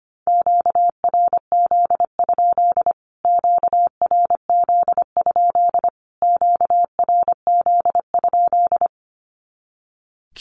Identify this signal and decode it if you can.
Morse code, heard as a keyed tone, about 25 wpm, 700 Hz. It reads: QRZ? QRZ? QRZ?
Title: Q-Codes Repeated 3x 25wpm